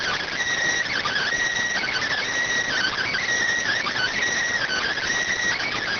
Emergency Alert System (SAME) tones, heard as harsh buzzing bursts. It is heard from 0.9 to 5.0 s.